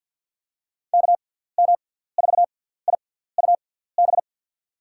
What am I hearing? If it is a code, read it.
Morse code, heard as a keyed tone, about 47 wpm, spaced as for 17 wpm, 700 Hz. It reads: KM4IUB
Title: USA Call Signs - 6x Character Spacing - 50wpm